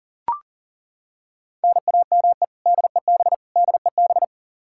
Morse code, heard as a keyed tone, about 40 wpm, 700 Hz. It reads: NAME DEB DEB
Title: Practice QSO - Exchange 1 - Beginning - 40wpm